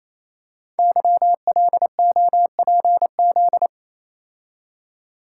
Morse code, heard as a keyed tone, about 29 wpm, 700 Hz. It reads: YLOPZ
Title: Code Groups - 5 Letters - 28wpm